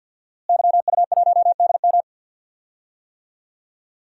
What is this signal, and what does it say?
Morse code, heard as a keyed tone, about 48 wpm, 700 Hz. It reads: YU1DM
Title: DXCC-100 Call Signs - 5 Characters - Repeated 3x - 50wpm